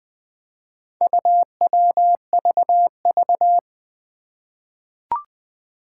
Morse code, heard as a keyed tone, about 20 wpm, 700 Hz. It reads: UWVV
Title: Four Challenging Letter-Number Combinations 20wpm